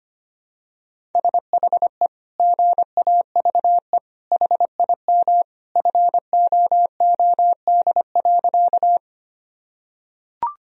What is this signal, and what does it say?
Morse code, heard as a keyed tone, about 25 wpm, 700 Hz. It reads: SHE GAVE HIM FOOD.
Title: Sentences from Top 1000 Words - Review Part 1 - 25wpm